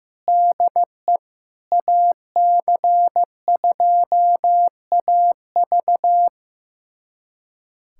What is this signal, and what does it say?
Morse code, heard as a keyed tone, about 16 wpm, 700 Hz. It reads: DE AC2AV
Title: USA Callsign DE Callsign - 5 Characters - 15wpm